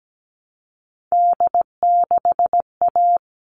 Morse code, heard as a keyed tone, about 17 wpm, 700 Hz. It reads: D6A
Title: All Cut Numbers - 3-Digit 17wpm